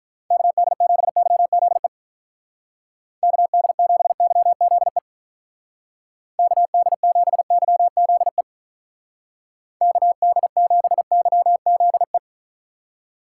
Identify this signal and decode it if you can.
Morse code, heard as a keyed tone, about 43 wpm, 700 Hz. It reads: KD7YZE KD7YZE KD7YZE KD7YZE
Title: Speed Racing - USA Call Signs 35wpm